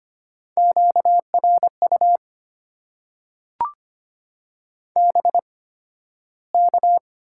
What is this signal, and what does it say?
Morse code, heard as a keyed tone, about 25 wpm, spaced as for 13 wpm, 690 Hz. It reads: QRU BK
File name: Top 90 QSO Elements - 8x Character Spacing - 25wpm